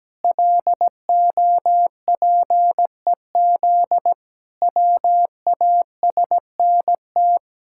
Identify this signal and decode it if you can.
Morse code, heard as a keyed tone, about 17 wpm, 700 Hz. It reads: LOPEZ WASNT
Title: Pangram Challenge v2 - 17wpm